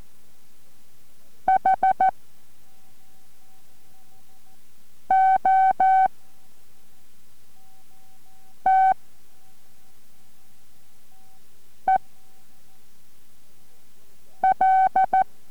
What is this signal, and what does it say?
Morse code, heard as a keyed tone, about 14 wpm, spaced as for 3 wpm, 760 Hz. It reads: HOTEL